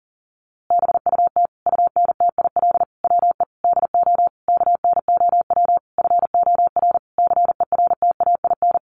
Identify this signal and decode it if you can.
Morse code, heard as a keyed tone, about 40 wpm, 700 Hz. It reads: BUT UNTIL WE DO KNOW FOR CERTAIN
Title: A Sound Of Thunder 40wpm